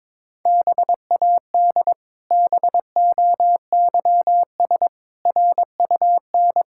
Morse code, heard as a keyed tone, about 22 wpm, 700 Hz. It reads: BAD BOYS RUN
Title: Pangram Challenge v2 - 22wpm